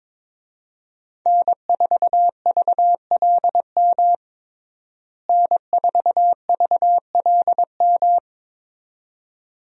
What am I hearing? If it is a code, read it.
Morse code, heard as a keyed tone, about 21 wpm, 700 Hz. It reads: N4VLM N4VLM
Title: USA Call Signs - 5 Characters - Repeated 2x - 22wpm